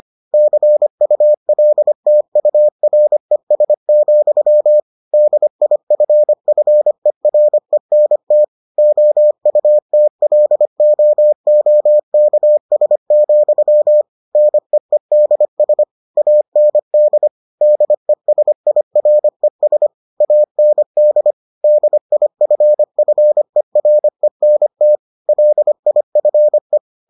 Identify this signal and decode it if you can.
Morse code, heard as a keyed tone, about 25 wpm, 600 Hz. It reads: CULTURES, DIFFERENT OUTLOOKS, NEEDS AND DESIRES AND DIFFERENT LIFE